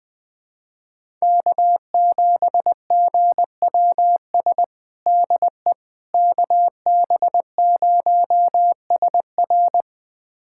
Morse code, heard as a keyed tone, about 20 wpm, 700 Hz. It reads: K7GWS DE KB0SR
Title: USA Callsign DE Callsign - 5 Characters - 20wpm